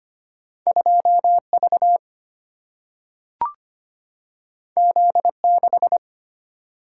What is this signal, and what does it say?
Morse code, heard as a keyed tone, about 25 wpm, 700 Hz. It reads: 2V Z6
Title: Single-Letter Single-Number 25wpm